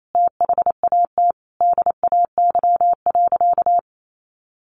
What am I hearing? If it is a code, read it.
Morse code, heard as a keyed tone, about 28 wpm, 700 Hz. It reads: THAT DAY.
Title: Sentences from Top 1500 Words 28wpm